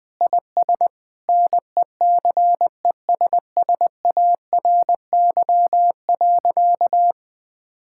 Morse code, heard as a keyed tone, about 20 wpm, 700 Hz. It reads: IS NECESSARY.
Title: Sentences from Top 1000 Words - Review Part 5 - 20wpm